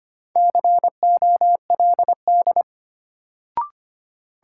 Morse code, heard as a keyed tone, about 25 wpm, 700 Hz. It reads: COLD
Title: Sets of 5 Words - New Word Drill - 25wpm